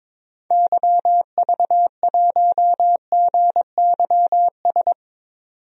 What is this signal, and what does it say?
Morse code, heard as a keyed tone, about 22 wpm, 700 Hz. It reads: YV1GYS
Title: DXCC-100 Call Signs - 6 Characters - 22wpm